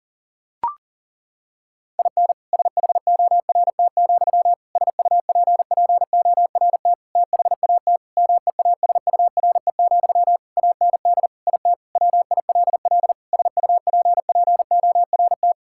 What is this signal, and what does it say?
Morse code, heard as a keyed tone, about 37 wpm, 700 Hz. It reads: IN SHORT, SUPPORT THAT MEASURE, AND IT WILL SUPPORT
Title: Sentences from Top 1000 Words - Review Part 5 - 40wpm